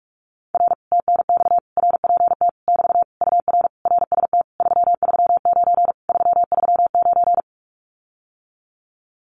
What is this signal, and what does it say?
Morse code, heard as a keyed tone, about 46 wpm, 700 Hz. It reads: R TNX RPT <BT> UR RST 339 339
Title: Common QSO Phrases Encore 45wpm